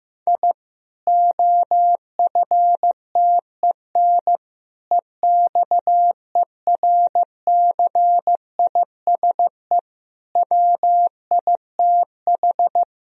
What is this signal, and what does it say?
Morse code, heard as a keyed tone, about 16 wpm, 700 Hz. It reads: I OFTEN EXERCISE WITH
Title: Sentences from Top 1000 Words - Review Part 4 - 15wpm